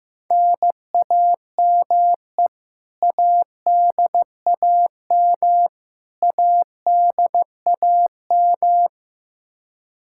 Morse code, heard as a keyed tone, about 15 wpm, 700 Hz. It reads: NAME ADAM ADAM
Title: K1USN SST QSO Practice 15wpm